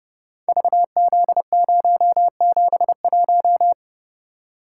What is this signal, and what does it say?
Morse code, heard as a keyed tone, about 29 wpm, 700 Hz. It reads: VZ071
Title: Code Groups - 5 Letter:Numbers - 30wpm